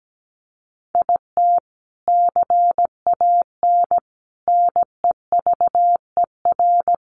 Morse code, heard as a keyed tone, about 17 wpm, 700 Hz. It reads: IT CAN NEVER
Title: Sentences from Top 400 Words 17wpm